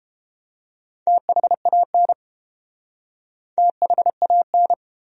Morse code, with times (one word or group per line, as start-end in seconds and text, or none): 1.1-2.1 THAN
3.6-4.7 THAN